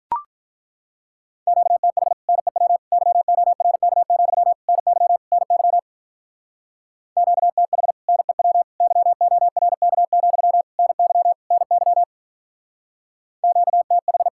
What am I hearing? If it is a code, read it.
Morse code, heard as a keyed tone, about 43 wpm, 700 Hz. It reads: QTH NEW YORK, NY NY QTH NEW YORK, NY NY QTH
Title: Speed-Racing Common QSO Phrases - 35wpm